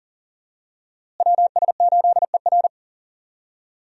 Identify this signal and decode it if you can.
Morse code, heard as a keyed tone, about 40 wpm, 700 Hz. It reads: WS8ER
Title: USA Call Signs - 5 Characters - Repeated 3x - 40wpm